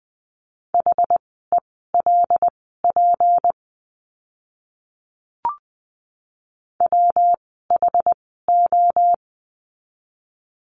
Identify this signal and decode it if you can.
Morse code, heard as a keyed tone, about 20 wpm, spaced as for 14 wpm, 700 Hz. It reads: HELP WHO
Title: 237 Lesson 38f - Letter q - Words 20wpm